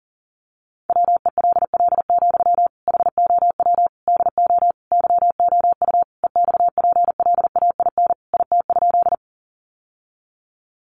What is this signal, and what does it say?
Morse code, heard as a keyed tone, about 40 wpm, 700 Hz. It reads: WELL, HOW DO YOU EXPLAIN IT?
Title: Sentences from Top 1100 Words 40wpm